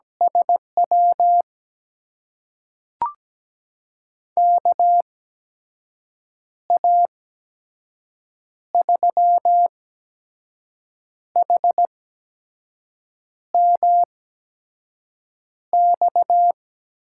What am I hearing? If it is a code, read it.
Morse code, heard as a keyed tone, about 17 wpm, spaced as for 5 wpm, 700 Hz. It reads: SW KA3HMX
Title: USA Call Signs - 6 Characters - 8x Character Spacing - 17wpm